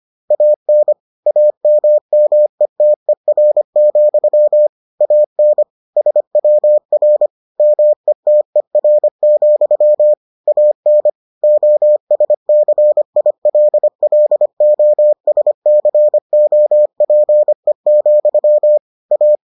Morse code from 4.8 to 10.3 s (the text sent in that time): AN SWR METER,